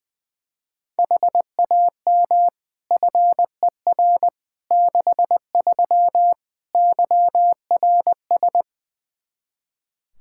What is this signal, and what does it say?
Morse code, heard as a keyed tone, about 20 wpm, 700 Hz. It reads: HAM FER 63 YRS
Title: Practice QSO - Exchange 3 - More Info - 20wpm